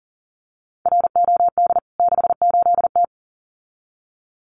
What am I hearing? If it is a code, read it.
Morse code, heard as a keyed tone, about 40 wpm, 700 Hz. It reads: ROD 68T
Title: CWT Exchange 40wpm